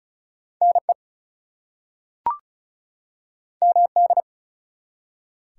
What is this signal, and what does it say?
Morse code, heard as a keyed tone, about 36 wpm, 700 Hz. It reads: NE MD
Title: US States and Canadian Provinces 35wpm